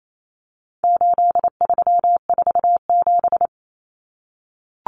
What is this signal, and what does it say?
Morse code, heard as a keyed tone, about 28 wpm, 700 Hz. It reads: 8347
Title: Numbers - 4-Digit 28wpm